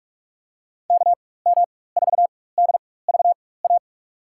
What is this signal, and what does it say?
Morse code, heard as a keyed tone, about 47 wpm, spaced as for 21 wpm, 700 Hz. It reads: KM4DVA
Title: USA Call Signs - 4x Character Spacing - 45wpm